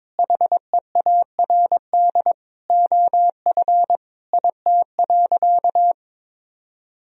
Morse code, heard as a keyed tone, about 22 wpm, 700 Hz. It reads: HEARD OF IT.
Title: Sentences from Top 1000 Words - Review Part 3 - 22wpm